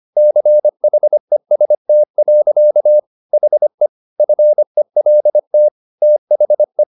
Morse code, heard as a keyed tone, about 25 wpm, 600 Hz. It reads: CHEST. HE FELT THE